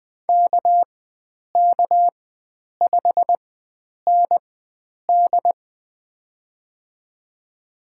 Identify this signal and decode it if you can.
Morse code, heard as a keyed tone, about 20 wpm, spaced as for 9 wpm, 700 Hz. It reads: KK5ND
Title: USA Call Signs - 5 Characters - 4x Character Spacing - 20wpm